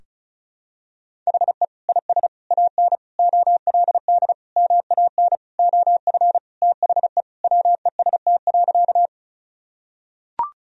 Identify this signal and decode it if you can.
Morse code, heard as a keyed tone, about 36 wpm, 700 Hz. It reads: HE IS AN OLD MAN OF THE WEST.
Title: Sentences from Top 1000 Words - Review Part 2 - 35wpm